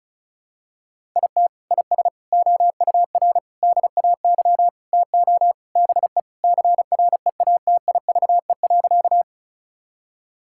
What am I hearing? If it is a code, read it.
Morse code, heard as a keyed tone, about 36 wpm, 700 Hz. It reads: IT IS OUR DAY TO BE CREATIVE.
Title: Sentences from Top 1000 Words - Review Part 5 - 35wpm